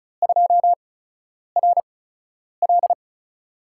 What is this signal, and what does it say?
Morse code, heard as a keyed tone, about 36 wpm, spaced as for 10 wpm, 700 Hz. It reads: 2RL
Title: USA Call Signs - 4 Characters - 8x Character Spacing - 35wpm